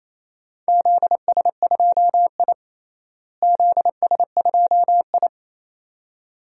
Morse code, heard as a keyed tone, about 28 wpm, 700 Hz. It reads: ZS2I ZS2I
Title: DXCC-100 Call Signs - 4 Characters - Repeated 2x - 28wpm